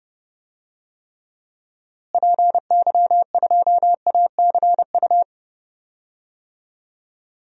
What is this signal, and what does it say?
Morse code, heard as a keyed tone, about 31 wpm, 700 Hz. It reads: PY2ACU
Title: DXCC-100 Call Signs - 6 Characters - Repeated 3x - 30wpm